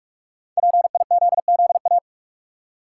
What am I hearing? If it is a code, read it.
Morse code, heard as a keyed tone, about 48 wpm, 700 Hz. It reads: PIZZA
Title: Five Letter Words - 45wpm